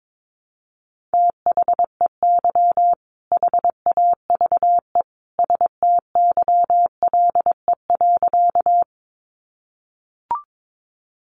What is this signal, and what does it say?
Morse code, heard as a keyed tone, about 22 wpm, 700 Hz. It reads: THEY HAVE STYLE.
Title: Sentences from Top 800 Words 22wpm